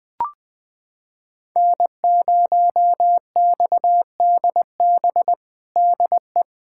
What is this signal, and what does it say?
Morse code, heard as a keyed tone, about 20 wpm, 700 Hz. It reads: N0XDB DE
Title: USA Callsign DE Callsign - 5 Characters - 20wpm